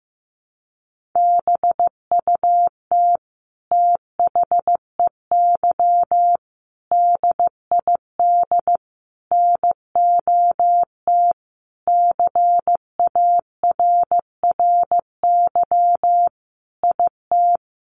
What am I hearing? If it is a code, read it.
Morse code, heard as a keyed tone, about 14 wpm, 700 Hz. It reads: BUT THEY DID NOT CARRY IT
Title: Sentences from Top 1400 Words 15wpm